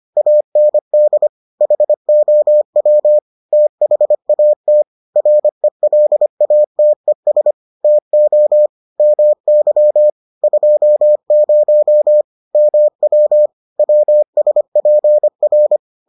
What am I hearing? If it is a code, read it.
Morse code, heard as a keyed tone, about 25 wpm, 600 Hz. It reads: AND HOW THAT RELATES TO MY 20 MW WSPR